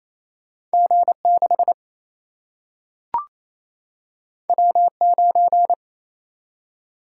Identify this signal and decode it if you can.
Morse code, heard as a keyed tone, about 28 wpm, 700 Hz. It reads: G6 W9
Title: Single-Letter Single-Number 28wpm